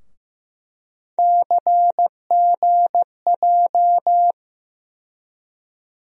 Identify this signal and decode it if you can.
Morse code, heard as a keyed tone, about 15 wpm, 700 Hz. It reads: CGJ
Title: USA Call Sign Postfixes - 3 Letters 15wpm